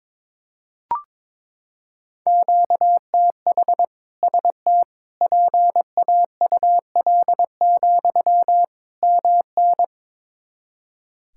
Morse code, heard as a keyed tone, about 22 wpm, 700 Hz. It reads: QTH ST PAUL, MN
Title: K1USN SST QSO Practice 22wpm